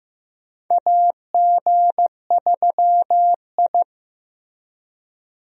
Morse code, heard as a keyed tone, about 16 wpm, 700 Hz. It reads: AG3I
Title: USA Call Signs - 4 Characters - Repeated 2x - 15wpm